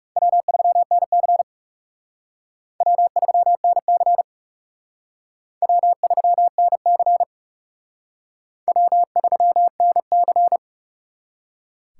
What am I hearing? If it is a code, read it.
Morse code, heard as a keyed tone, about 36 wpm, 700 Hz. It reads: W3NC W3NC W3NC W3NC
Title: Speed Racing - USA Call Signs 30wpm